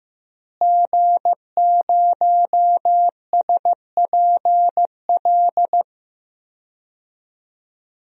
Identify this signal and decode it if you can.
Morse code, heard as a keyed tone, about 15 wpm, 700 Hz. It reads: G0SPL
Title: DXCC-100 Call Signs - 5 Characters - Repeated 3x - 15wpm